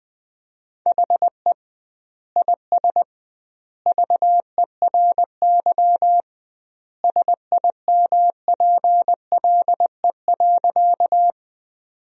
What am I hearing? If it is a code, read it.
Morse code, heard as a keyed tone, about 20 wpm, 700 Hz. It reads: HE IS VERY SIMPLE.
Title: 2x Word-Spacing - Sentences from Top 400 Words - 20wpm